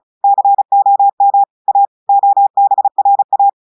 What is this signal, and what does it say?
Morse code, heard as a keyed tone, about 33 wpm, 800 Hz. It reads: COM A OBRA
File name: audios / portugais / portugais texte en clair 1 vitesse 35 mots minutes